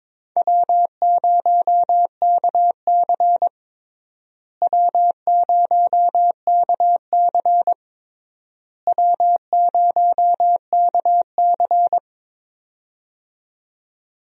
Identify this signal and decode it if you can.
Morse code, heard as a keyed tone, about 22 wpm, 700 Hz. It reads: W0KC W0KC W0KC
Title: USA Call Signs - 4 Characters - Repeated 3x - 22wpm